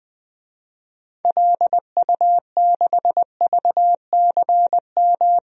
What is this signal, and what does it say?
Morse code, heard as a keyed tone, about 20 wpm, 700 Hz. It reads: LU6VCM
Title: DXCC-100 Call Signs - 6 Characters - 20wpm